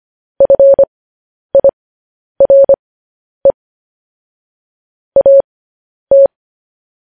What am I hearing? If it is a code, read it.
Morse code, heard as a keyed tone, about 25 wpm, spaced as for 10 wpm, 550 Hz. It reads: FIRE AT
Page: news-headlines-in-morse-code-at-10-wpm